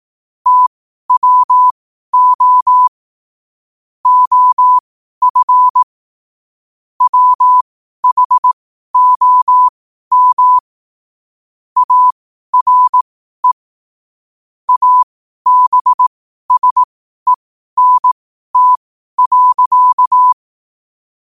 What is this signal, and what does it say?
Morse code, heard as a keyed tone, about 18 wpm, spaced as for 12 wpm, 1000 Hz. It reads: TWO OF WHOM ARE ABSENT.
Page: New quotes every day in morse code at 12 Words per minute.